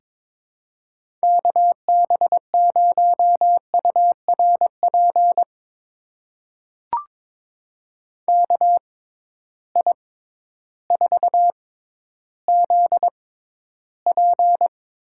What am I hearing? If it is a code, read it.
Morse code, heard as a keyed tone, about 21 wpm, spaced as for 12 wpm, 700 Hz. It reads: KB0URP KI4ZP
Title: USA Call Signs - 6 Characters - 6x Character Spacing - 22wpm